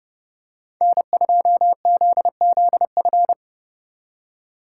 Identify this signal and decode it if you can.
Morse code, heard as a keyed tone, about 30 wpm, 700 Hz. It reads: N2ZZF